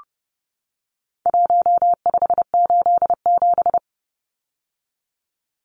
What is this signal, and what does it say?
Morse code, heard as a keyed tone, about 30 wpm, 700 Hz. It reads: 1587
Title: Numbers - Contest Style - 4-Digit 30wpm